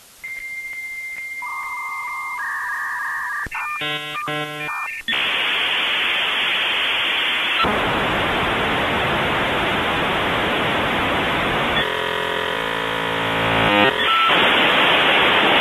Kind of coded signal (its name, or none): dial-up modem handshake